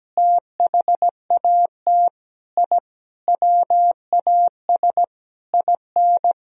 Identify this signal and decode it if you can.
Morse code, heard as a keyed tone, about 17 wpm, 700 Hz. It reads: THAT I WAS IN
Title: Sentences from Top 1000 Words - Review Part 1 - 17wpm